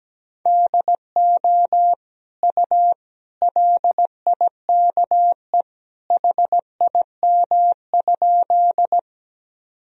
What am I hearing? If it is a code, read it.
Morse code, heard as a keyed tone, about 17 wpm, 700 Hz. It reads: DO U LIKE HIM?
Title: Sentences from Top 100 Words and Abbreviations 17wpm